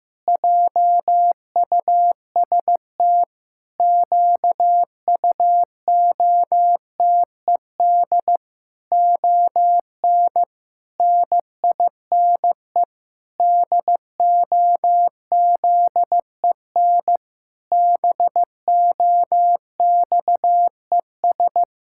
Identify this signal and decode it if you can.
Morse code, heard as a keyed tone, about 15 wpm, 700 Hz. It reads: JUST QUOTED ON NINE DOZEN BOXES
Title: Pangram Challenge v2 - 15wpm